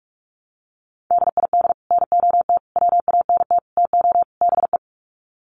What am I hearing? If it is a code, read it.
Morse code, heard as a keyed tone, about 44 wpm, 700 Hz. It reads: DID NOT WANT TO BE
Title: Sets of 5 Words Encore 45wpm